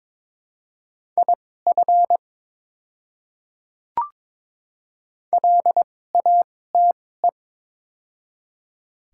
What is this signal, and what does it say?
Morse code, heard as a keyed tone, about 22 wpm, spaced as for 16 wpm, 700 Hz. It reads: IF LATE
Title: 2x Character Spacing - Top 300 Words - 22wpm